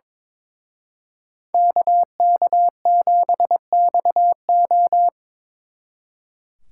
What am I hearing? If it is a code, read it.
Morse code, heard as a keyed tone, about 26 wpm, 700 Hz. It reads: KK7XO
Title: Speed Racing - USA 2x2 Call Signs 22wpm